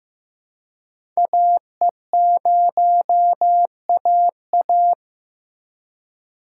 Morse code, heard as a keyed tone, about 15 wpm, 700 Hz. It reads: AE0AA
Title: USA Call Signs - 5 Characters - 4x Character Spacing - 15wpm